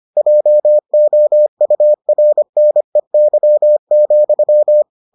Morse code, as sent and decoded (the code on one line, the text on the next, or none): .--- --- ..- .-. -. -.-- --..--
JOURNEY,